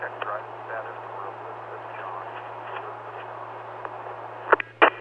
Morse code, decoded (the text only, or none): none